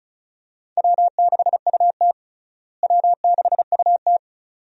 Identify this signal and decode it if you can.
Morse code, heard as a keyed tone, about 34 wpm, 700 Hz. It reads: W6UT W6UT
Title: USA Call Signs - 4 Characters - Repeated 2x - 35wpm